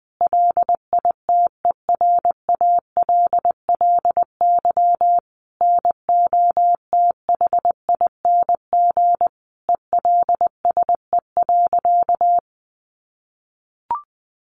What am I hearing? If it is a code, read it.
Morse code, heard as a keyed tone, about 20 wpm, 700 Hz. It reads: LITERALLY NOTHING ELSE.
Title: Sentences from Top 1400 Words 20wpm